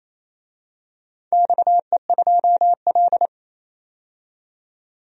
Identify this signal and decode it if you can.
Morse code, heard as a keyed tone, about 28 wpm, 700 Hz. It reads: XE2L
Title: DXCC-100 Call Signs - 4 Characters - 28wpm